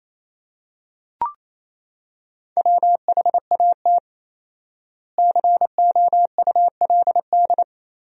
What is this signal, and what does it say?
Morse code, heard as a keyed tone, about 28 wpm, 700 Hz. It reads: WHAT COULD
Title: 4x Word-Spacing - Sentences from Top 400 Words - 28wpm